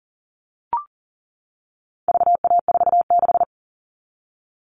Morse code, heard as a keyed tone, about 40 wpm, 700 Hz. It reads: VA46
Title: All Cut Numbers - 4-Digit 40wpm